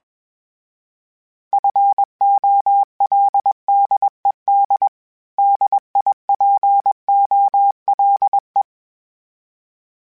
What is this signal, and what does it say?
Morse code, heard as a keyed tone, about 21 wpm, 800 Hz. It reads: FOLDED DIPOLE
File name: fromSMs / FromSMs mp3s / Antennas@20